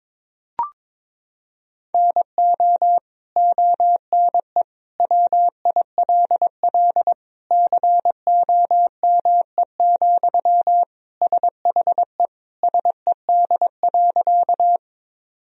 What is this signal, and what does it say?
Morse code, heard as a keyed tone, about 22 wpm, 700 Hz. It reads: NO ONE WILL COME, SHE SED.
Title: Sentences from Top 100 Words and Abbreviations 22wpm